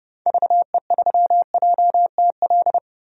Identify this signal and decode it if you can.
Morse code, heard as a keyed tone, about 30 wpm, 700 Hz. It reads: VE3JTL
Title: DXCC-100 Call Signs - 6 Characters - 30wpm